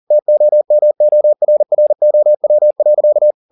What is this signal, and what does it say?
Morse code, heard as a keyed tone, about 40 wpm, 600 Hz. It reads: TOMORROW.
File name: the old man and the sea 2 40wpm